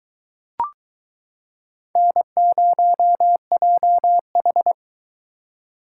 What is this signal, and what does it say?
Morse code, heard as a keyed tone, about 22 wpm, 700 Hz. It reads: N0JH